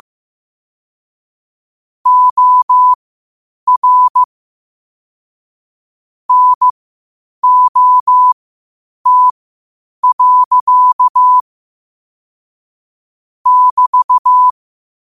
Morse code, heard as a keyed tone, about 15 wpm, spaced as for 8 wpm, 1000 Hz. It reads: OR NOT. <BT>
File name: qotdfiles / 8wpm